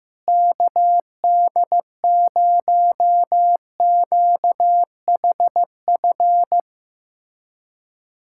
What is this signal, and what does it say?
Morse code, heard as a keyed tone, about 15 wpm, 700 Hz. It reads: KD0QHF
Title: USA Callsign DE Callsign - 6 Characters - 15wpm